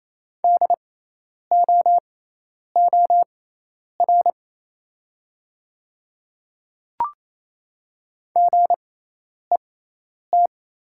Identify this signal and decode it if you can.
Morse code, heard as a keyed tone, about 28 wpm, spaced as for 10 wpm, 700 Hz. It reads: DOOR GET
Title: 6x Character Spacing - Top 300 Words - 28wpm